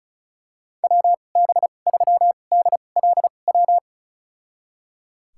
Morse code, heard as a keyed tone, about 34 wpm, spaced as for 25 wpm, 700 Hz. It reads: WB3DLW
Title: USA Call Signs - 6 Characters - 2x Character Spacing - 35wpm